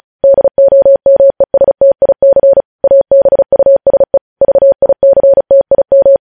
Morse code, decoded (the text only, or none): DOMESTIC ABUSE VICTIM